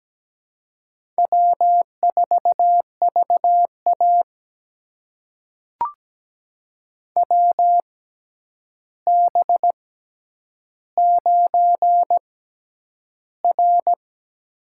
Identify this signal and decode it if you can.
Morse code, heard as a keyed tone, about 17 wpm, spaced as for 9 wpm, 700 Hz. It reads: W4VA WB9R
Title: USA Call Signs - 4 Characters - 6x Character Spacing - 17wpm